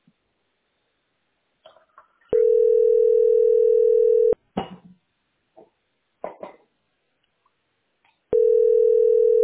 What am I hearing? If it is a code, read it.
Ringback tone (North America), 440 + 480 Hz.